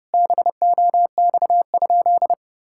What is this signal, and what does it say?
Morse code, heard as a keyed tone, about 30 wpm, 700 Hz. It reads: BOX?
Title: Sentences from Top 1000 Words - Review Part 5 - 30wpm